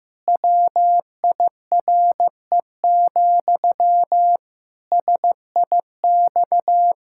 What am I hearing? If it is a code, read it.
Morse code, heard as a keyed tone, about 15 wpm, 700 Hz. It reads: WIRE, SIX
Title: Pangram Challenge v2 - 15wpm